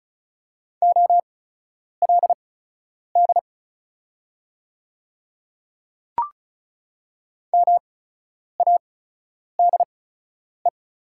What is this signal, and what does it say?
Morse code, heard as a keyed tone, about 36 wpm, spaced as for 10 wpm, 700 Hz. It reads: OLD MADE
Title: 8x Character Spacing - Top 200 Words - 35wpm